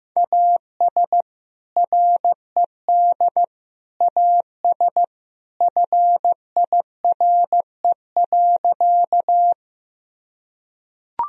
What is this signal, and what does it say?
Morse code, heard as a keyed tone, about 15 wpm, 700 Hz. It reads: AS RED AS FIRE.
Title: Sentences from Top 1000 Words - Review Part 2 - 15wpm